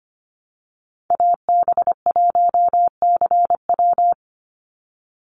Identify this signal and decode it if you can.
Morse code, heard as a keyed tone, about 25 wpm, 700 Hz. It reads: AB1CW
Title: Call Signs Repeated 3x - 25wpm